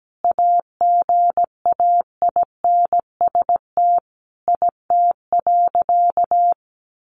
Morse code, heard as a keyed tone, about 17 wpm, 700 Hz. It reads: AGAINST IT.
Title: Sentences from Top 1500 Words 17wpm